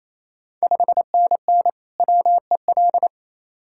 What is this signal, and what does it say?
Morse code, heard as a keyed tone, about 27 wpm, 690 Hz. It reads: 5NN WEL
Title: Ontario QSO Party Exchange 28wpm